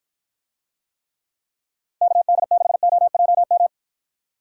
Morse code, heard as a keyed tone, about 53 wpm, 700 Hz. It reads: KD6OJM